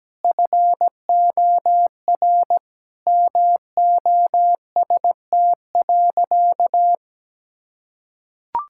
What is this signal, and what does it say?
Morse code, heard as a keyed tone, about 17 wpm, 700 Hz. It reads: FOR MOST.
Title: Sentences from Top 1600 Words 17wpm